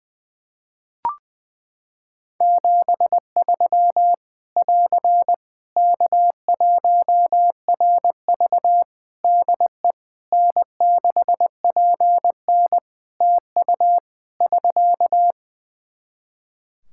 Morse code, heard as a keyed tone, about 20 wpm, 700 Hz. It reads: 73 <AR> K1RV DE N6PN TU <SK>
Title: Practice QSO - Exchange 4 - Ending - 20wpm